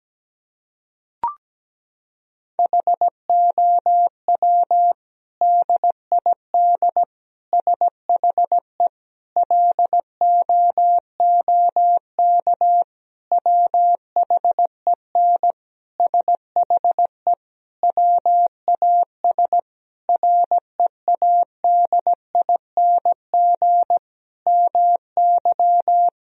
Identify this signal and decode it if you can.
Morse code, heard as a keyed tone, about 17 wpm, 700 Hz. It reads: HOW DID SHE LOOK WHEN SHE WAS READING MY
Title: Sentences from Top 1000 Words - Review Part 3 - 17wpm